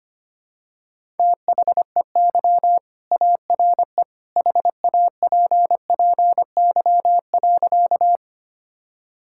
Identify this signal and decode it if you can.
Morse code, heard as a keyed tone, about 25 wpm, 700 Hz. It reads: THEY ARE HAPPY.
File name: practice / 25wpm / Sentences with 2-Word Phrases - 25wpm